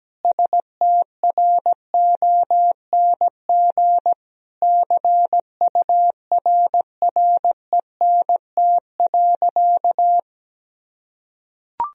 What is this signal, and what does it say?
Morse code, heard as a keyed tone, about 17 wpm, 700 Hz. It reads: STRONG CURRENT.
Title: Sentences from Top 1000 Words - Review Part 4 - 17wpm